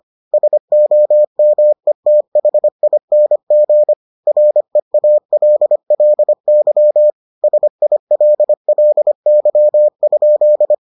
Morse code, heard as a keyed tone, about 25 wpm, 600 Hz. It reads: SOMETHING REALLY SILLY?